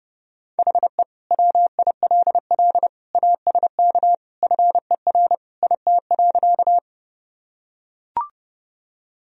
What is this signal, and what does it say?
Morse code, heard as a keyed tone, about 32 wpm, 700 Hz. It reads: HE WILL ASK FER IT.
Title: Sentences from Top 200 Words and Abbreviations 30wpm